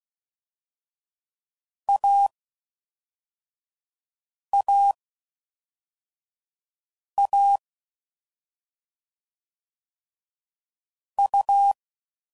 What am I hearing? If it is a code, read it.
Morse code, heard as a keyed tone, about 16 wpm, 800 Hz. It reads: A A A U
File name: Combinations / Difficult Combos / AUV AUV4